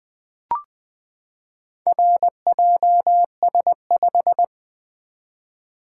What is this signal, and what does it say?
Morse code, heard as a keyed tone, about 20 wpm, 700 Hz. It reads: RJS5
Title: Four Letter-Number Combinations - 20wpm